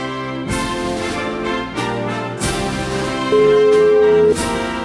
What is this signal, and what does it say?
Ringback tone (Europe, 425 Hz), with noise.